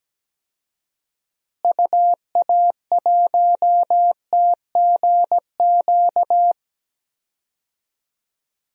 Morse code, heard as a keyed tone, about 17 wpm, 700 Hz. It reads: UA1TGQ
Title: DXCC-100 Call Signs - 6 Characters - Repeated 2x - 17wpm